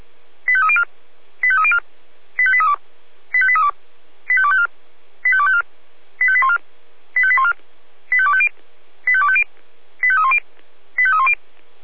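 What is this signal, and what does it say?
Five-tone selective call (ZVEI), heard as quick runs of five tones, tones of about 80 ms. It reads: 85304 85304 85832 85832 85254 85254 85814 85814 85250 85250 85310 85310
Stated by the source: ZVEI five-tone call 85201 received via SDR.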